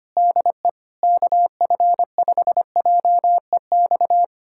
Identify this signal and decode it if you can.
Morse code, heard as a keyed tone, about 25 wpm, 700 Hz. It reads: DE KF5JEX
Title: USA Callsign DE Callsign - 6 Characters - 25wpm